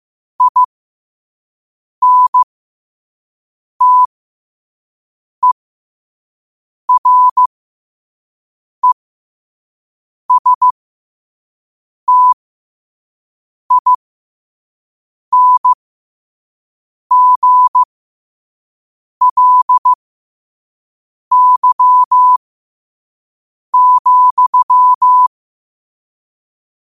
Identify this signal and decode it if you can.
Morse code, heard as a keyed tone, about 15 wpm, spaced as for 5 wpm, 1000 Hz. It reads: INTERESTINGLY,
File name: qotdfiles / 5wpm